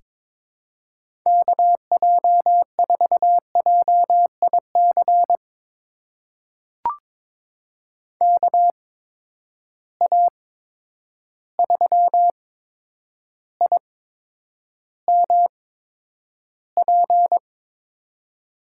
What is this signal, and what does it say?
Morse code, heard as a keyed tone, about 22 wpm, spaced as for 9 wpm, 700 Hz. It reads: KJ4JIC KA3IMP
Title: USA Call Signs - 6 Characters - 8x Character Spacing - 22wpm